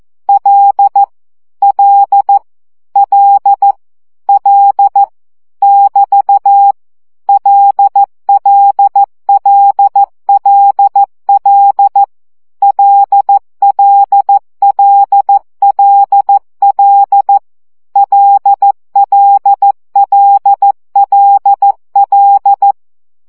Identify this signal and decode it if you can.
Morse code, heard as a keyed tone, about 14 wpm, 800 Hz. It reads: L L L L <BT> LLLLL LLLLL LLLLL
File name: CW gyakorlas l 12 baud